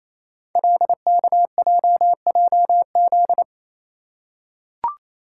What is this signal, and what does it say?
Morse code, heard as a keyed tone, about 27 wpm, 700 Hz. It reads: LKJJZ
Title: Code Groups - 5 Letters - 28wpm